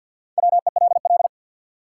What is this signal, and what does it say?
Morse code, heard as a keyed tone, about 53 wpm, 700 Hz. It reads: WELL
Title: Sets of 2 Words - New Word Drill - 50wpm